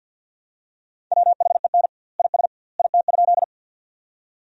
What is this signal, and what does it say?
Morse code, heard as a keyed tone, about 50 wpm, 700 Hz. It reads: WHEN IS IT?
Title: 2x Word-Spacing - Sentences from Top 100 Words - 50wpm